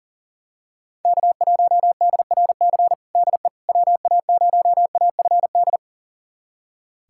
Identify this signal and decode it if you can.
Morse code, heard as a keyed tone, about 40 wpm, 700 Hz. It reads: K1DRC DE WA0AFD
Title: USA Callsign DE Callsign 40wpm